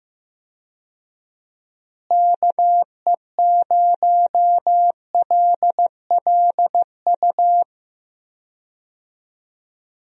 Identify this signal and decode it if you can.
Morse code, heard as a keyed tone, about 16 wpm, 700 Hz. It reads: KE0LLU
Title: USA Call Signs - 6 Characters - Repeated 3x - 15wpm